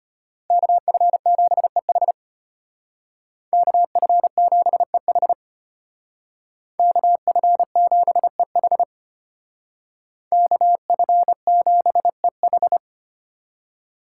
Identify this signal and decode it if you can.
Morse code, heard as a keyed tone, about 30 wpm, 700 Hz. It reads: KF7EH KF7EH KF7EH KF7EH
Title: Speed Racing - USA 2x2 Call Signs 25wpm